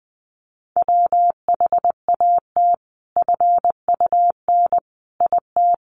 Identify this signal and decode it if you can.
Morse code, heard as a keyed tone, about 20 wpm, 700 Hz. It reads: WHAT FUN IT
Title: Sentences from Top 700 Words 20wpm